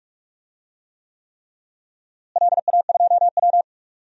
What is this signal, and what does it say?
Morse code, heard as a keyed tone, about 46 wpm, 700 Hz. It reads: LA2W
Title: DXCC-100 Call Signs - 4 Characters - Repeated 3x - 45wpm